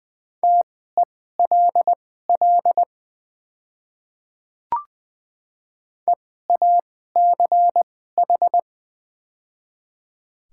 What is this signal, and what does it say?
Morse code, heard as a keyed tone, about 20 wpm, spaced as for 15 wpm, 700 Hz. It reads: TELL EACH
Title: 2x Character Spacing - Top 200 Words - 20wpm